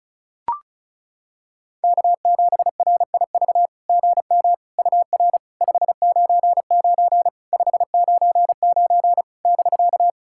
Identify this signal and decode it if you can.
Morse code, heard as a keyed tone, about 34 wpm, 690 Hz. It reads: K7RIV GM UR 599 599 <BK>
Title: SOTA QSO Practice 35wpm